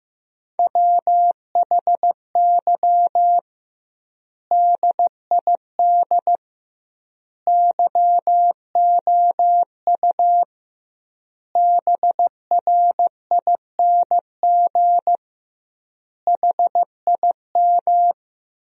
Morse code, heard as a keyed tone, about 15 wpm, 700 Hz. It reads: WHY DID YOU BRING HIM
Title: 2x Word-Spacing - Sentences from Top 500 Words - 15wpm